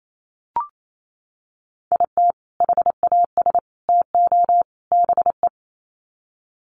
Morse code, heard as a keyed tone, about 28 wpm, 700 Hz. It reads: IT HAS TO BE
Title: Sets of 4 Words 28wpm